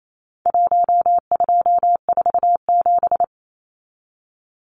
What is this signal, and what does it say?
Morse code, heard as a keyed tone, about 28 wpm, 700 Hz. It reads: 1247
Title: Numbers - 4-Digit 28wpm